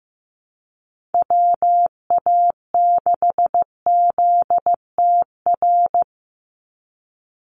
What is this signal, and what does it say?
Morse code, heard as a keyed tone, about 15 wpm, 700 Hz. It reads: WA6ZTR
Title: USA Call Signs 15wpm